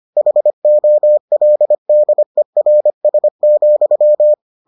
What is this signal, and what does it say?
Morse code, heard as a keyed tone, about 25 wpm, 600 Hz. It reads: HOLDERS,